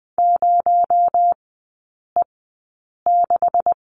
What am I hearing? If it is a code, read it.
Morse code, heard as a keyed tone, about 20 wpm, spaced as for 8 wpm, 700 Hz. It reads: 0E6
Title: 218 Lesson 35e - Number 0 - Three Characters 20wpm